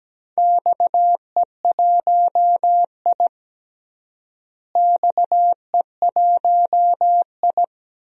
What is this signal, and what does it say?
Morse code, heard as a keyed tone, about 17 wpm, 700 Hz. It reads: XE1I XE1I
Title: DXCC-100 Call Signs - 4 Characters - Repeated 3x - 17wpm